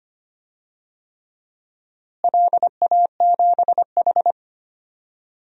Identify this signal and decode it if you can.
Morse code, heard as a keyed tone, about 25 wpm, 700 Hz. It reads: LA7H
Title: DXCC-100 Call Signs - 4 Characters - Repeated 3x - 25wpm